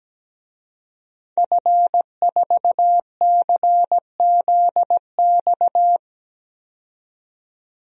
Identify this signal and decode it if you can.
Morse code, heard as a keyed tone, about 17 wpm, 700 Hz. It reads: F4CZX
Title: DXCC-100 Call Signs - 5 Characters - 17wpm